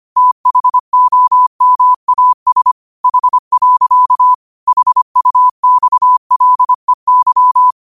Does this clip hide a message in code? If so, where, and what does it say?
Yes: Morse code from 0.2 to 7.7 s, heard as a keyed tone, about 25 wpm, 1000 Hz. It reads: THOMAS H. HUXLEY